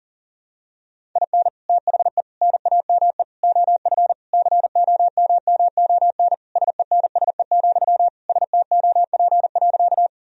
Morse code, heard as a keyed tone, about 40 wpm, 700 Hz. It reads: IN THE NAME OF COMMON SENSE, STOP.
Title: Sentences from Top 1000 Words - Review Part 4 - 40wpm